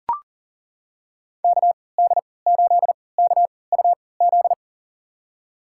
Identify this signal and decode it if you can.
Morse code, heard as a keyed tone, about 39 wpm, spaced as for 22 wpm, 700 Hz. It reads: KD8KUZ
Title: USA Call Signs - 6 Characters - 3x Character Spacing - 40wpm